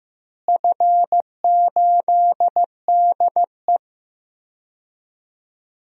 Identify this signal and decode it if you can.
Morse code, heard as a keyed tone, about 15 wpm, 700 Hz. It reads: F8DE
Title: DXCC-100 Call Signs - 4 Characters - Repeated 3x - 15wpm